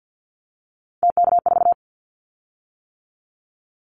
Morse code, heard as a keyed tone, about 50 wpm, 700 Hz. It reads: TK4
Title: Callsign Prefixes for Top 100 DXCC - 50wpm